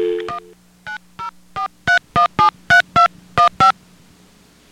DTMF, heard as telephone touch tones, tones of about 120 ms. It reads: 1 9043173215
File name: BellSouthTest Star67